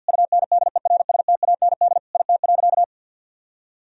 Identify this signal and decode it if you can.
Morse code, heard as a keyed tone, about 50 wpm, 700 Hz. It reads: UNDERSTAND IT.
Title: Sentences from Top 1000 Words - Review Part 3 - 50wpm